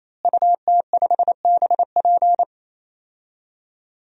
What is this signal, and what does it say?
Morse code, heard as a keyed tone, about 28 wpm, 700 Hz. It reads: UT5BP
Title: DXCC-100 Call Signs - 5 Characters - Repeated 3x - 28wpm